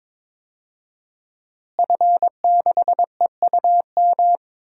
Morse code, heard as a keyed tone, about 22 wpm, 700 Hz. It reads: F6EUM
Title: DXCC-100 Call Signs - 5 Characters - Repeated 2x - 22wpm